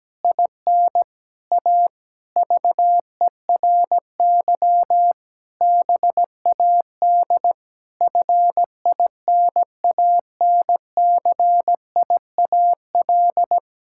Morse code, heard as a keyed tone, about 17 wpm, 700 Hz. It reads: IN A VERY BAD FINANCIAL
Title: Sentences from Top 1000 Words - Review Part 5 - 17wpm